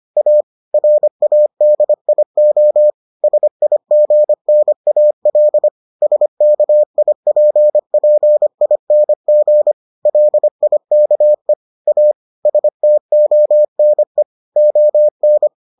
Morse code, heard as a keyed tone, about 25 wpm, 600 Hz. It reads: A RADIO SIGNAL SKIPPING LIKE A STONE ON